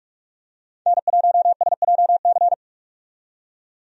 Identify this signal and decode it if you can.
Morse code, heard as a keyed tone, about 46 wpm, 700 Hz. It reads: N1SJC
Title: USA Call Signs - 5 Characters - 45wpm